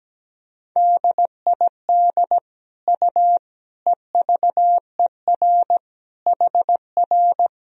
Morse code, heard as a keyed tone, about 17 wpm, 700 Hz. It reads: DID U EVER HR
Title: Sentences from Top 400 Words and Abbreviations 17wpm